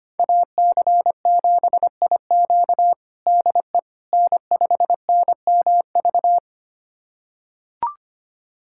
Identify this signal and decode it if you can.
Morse code, heard as a keyed tone, about 25 wpm, 700 Hz. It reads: AC7IQ DE N5NMV
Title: USA Callsign DE Callsign - 5 Characters - 25wpm